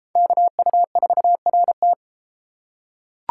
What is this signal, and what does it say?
Morse code, heard as a keyed tone, about 32 wpm, 700 Hz. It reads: KU4RT